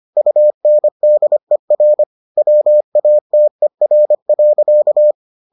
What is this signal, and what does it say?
Morse code, heard as a keyed tone, about 25 wpm, 600 Hz. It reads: UNDER WATER.